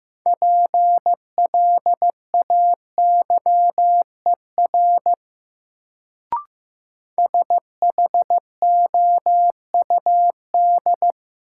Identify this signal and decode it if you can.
Morse code, heard as a keyed tone, about 15 wpm, 700 Hz. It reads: PLAYER SHOUD
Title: Sentences from Top 1000 Words - Review Part 4 - New Word Drill - 15wpm